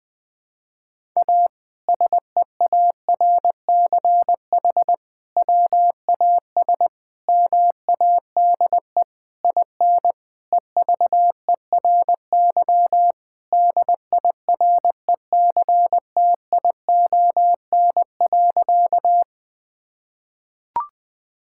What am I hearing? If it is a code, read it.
Morse code, heard as a keyed tone, about 20 wpm, 700 Hz. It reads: A SEARCH WAS MADE IN EVERY DIRECTION.
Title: Sentences from Top 1000 Words - Review Part 5 - 20wpm